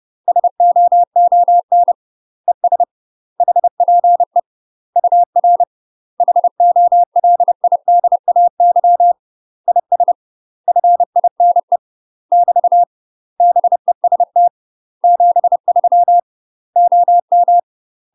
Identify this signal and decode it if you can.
Morse code, heard as a keyed tone, about 30 wpm, 700 Hz. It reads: SOON ES HPE UR HOLIDAY IS FINE <BT> BEST 73 OM
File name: UK MASTER QSOs TRAINING / 30WPM / UK MASTER QSOs 0013